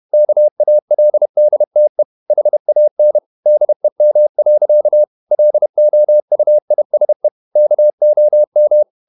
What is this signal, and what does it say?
Morse code, heard as a keyed tone, about 31 wpm, 600 Hz. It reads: KALDTE HAN DEM. LOUISE KOM
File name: Ferie QRQ 31wpm